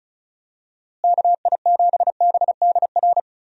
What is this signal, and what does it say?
Morse code, heard as a keyed tone, about 36 wpm, 700 Hz. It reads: KI7BDR
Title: USA Call Signs - 6 Characters - Repeated 3x - 35wpm